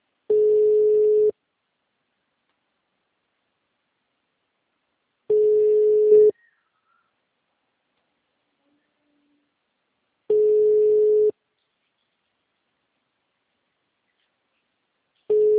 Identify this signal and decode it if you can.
Ringback tone (Europe, 425 Hz).